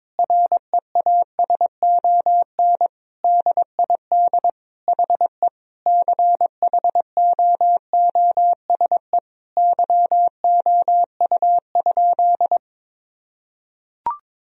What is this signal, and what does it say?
Morse code, heard as a keyed tone, about 22 wpm, 700 Hz. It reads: REASON DID HE CHOOSE YOU?
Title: Sentences from Top 1000 Words - Review Part 4 - 22wpm